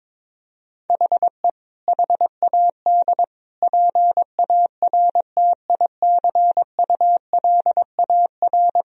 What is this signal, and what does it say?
Morse code, heard as a keyed tone, about 21 wpm, 700 Hz. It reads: HE HAD PARTICULAR
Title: Sentences from Top 1000 Words - Review Part 4 - 22wpm